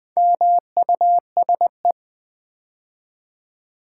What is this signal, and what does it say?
Morse code, heard as a keyed tone, about 20 wpm, 700 Hz. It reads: MUSE
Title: Build A Word - 2 Letters At A Time - 6-Letter Words - 20wpm